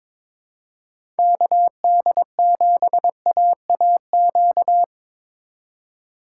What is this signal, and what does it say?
Morse code, heard as a keyed tone, about 22 wpm, 700 Hz. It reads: KD7AAQ
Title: USA Call Signs - 2x Character Spacing - 22wpm